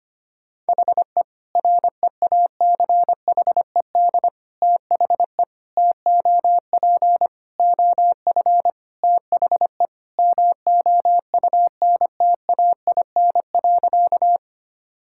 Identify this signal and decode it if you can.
Morse code, heard as a keyed tone, about 25 wpm, 700 Hz. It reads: HE REACHED THE TOP OF THE MOUNTAIN.
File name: practice / 25wpm / Sentences with 3-Word Phrases - 25wpm